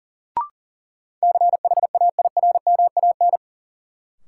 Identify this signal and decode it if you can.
Morse code, heard as a keyed tone, about 40 wpm, 700 Hz. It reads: CHAIRMAN
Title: Sets of 3 Words - New Word Drill - 40wpm